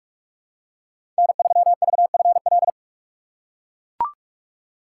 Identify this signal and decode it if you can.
Morse code, heard as a keyed tone, about 41 wpm, 700 Hz. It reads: N3VFL
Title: USA Call Signs - 5 Characters - 3x Character Spacing - 45wpm